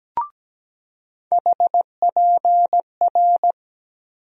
Morse code, heard as a keyed tone, about 17 wpm, 700 Hz. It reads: HPR
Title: USA Call Sign Postfixes - 3 Letters 17wpm